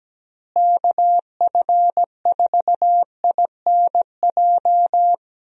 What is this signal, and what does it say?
Morse code, heard as a keyed tone, about 17 wpm, 700 Hz. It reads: KF4INJ
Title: USA Call Signs - 6 Characters - Repeated 3x - 17wpm